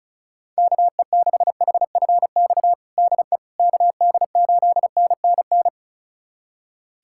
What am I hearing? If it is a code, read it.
Morse code, heard as a keyed tone, about 36 wpm, 700 Hz. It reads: KE6HFX DE KD8NNN
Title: USA Callsign DE Callsign - 6 Characters - 35wpm